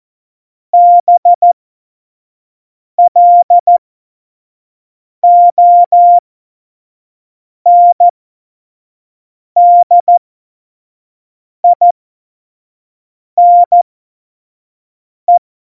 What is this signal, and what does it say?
Morse code, heard as a keyed tone, about 14 wpm, spaced as for 5 wpm, 700 Hz. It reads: BLONDINE
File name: Morsekurs / Witze / Witz 18 Juli 17, Tempo 25 F70